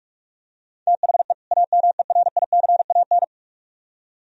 Morse code, heard as a keyed tone, about 44 wpm, 700 Hz. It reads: THE AMERICAN